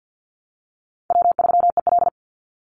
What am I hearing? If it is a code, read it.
Morse code, heard as a keyed tone, about 50 wpm, 700 Hz. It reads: W3EL
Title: Easy Call Signs - 50wpm